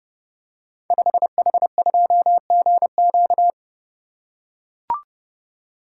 Morse code, heard as a keyed tone, about 30 wpm, 700 Hz. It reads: 5H2GQ
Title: Code Groups - 5 Letter:Numbers - 30wpm